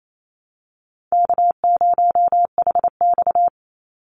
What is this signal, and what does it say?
Morse code, heard as a keyed tone, about 28 wpm, 700 Hz. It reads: K0HX
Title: Call Signs Repeated 2x - 28wpm